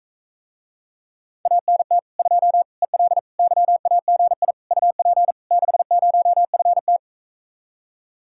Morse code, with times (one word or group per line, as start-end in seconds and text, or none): 1.5-2.0 ANT
2.2-2.6 2
2.8-3.2 EL
3.4-4.5 YAGI
4.7-5.3 UP
5.5-7.0 60FT